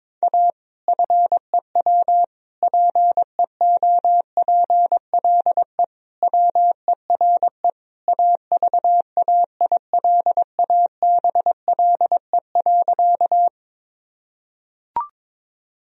Morse code, heard as a keyed tone, about 22 wpm, 700 Hz. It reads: A FEW PEOPLE WERE AVAILABLE.
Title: Sentences from Top 1000 Words - Review Part 3 - 22wpm